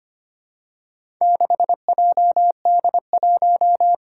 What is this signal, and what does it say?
Morse code, heard as a keyed tone, about 25 wpm, 700 Hz. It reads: 6JD1
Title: Four Challenging Letter-Number Combinations 25wpm